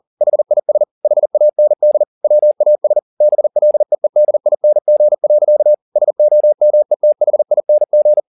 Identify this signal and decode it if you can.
Morse code, heard as a keyed tone, about 40 wpm, 600 Hz. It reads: HIS HAND WAS BLEEDING. SOMETHING